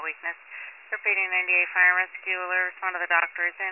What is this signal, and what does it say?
Emergency Alert System (SAME) tones, heard as harsh buzzing bursts, with speech. It is heard from 0.4 to 3.4 s.